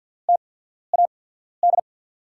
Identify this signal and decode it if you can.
Morse code, heard as a keyed tone, about 50 wpm, spaced as for 14 wpm, 700 Hz. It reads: TAD
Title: USA Call Signs - 6 Characters - 8x Character Spacing - 50wpm